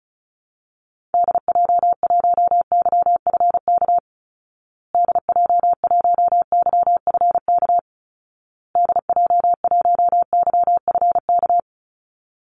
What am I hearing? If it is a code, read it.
Morse code, heard as a keyed tone, about 34 wpm, 700 Hz. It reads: DJ1YFK DJ1YFK DJ1YFK
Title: Call Signs Repeated 2x - 35wpm